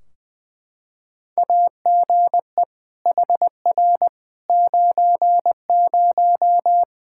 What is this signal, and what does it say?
Morse code, heard as a keyed tone, about 20 wpm, 700 Hz. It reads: AGE HR 90
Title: K1USN SST QSO Practice 20wpm